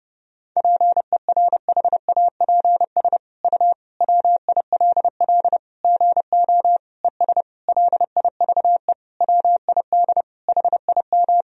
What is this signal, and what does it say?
Morse code, heard as a keyed tone, about 30 wpm, 700 Hz. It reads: PERHAPS U WILL GO ES LIVE WID HIM
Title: Sentences from Top 500 Words and Abbreviations 30wpm